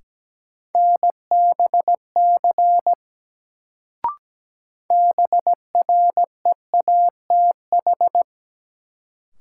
Morse code, heard as a keyed tone, about 17 wpm, 700 Hz. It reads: NBC BREATH
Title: Sets of 5 Words Encore - New Word Drill - 17wpm